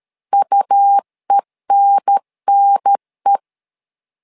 Morse code, heard as a keyed tone, about 13 wpm, 800 Hz. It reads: UENNE
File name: LEZIONI / 2 UHDN / 2 UHDN 12wpm 1456